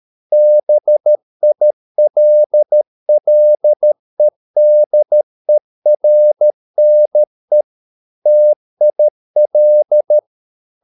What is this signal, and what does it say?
Morse code, heard as a keyed tone, about 13 wpm, 600 Hz. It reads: BILLEDERNE TIL
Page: Ferie 13 wpm